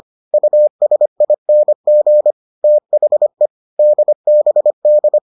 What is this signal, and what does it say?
Morse code, heard as a keyed tone, about 25 wpm, 600 Hz. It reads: USING THE DBD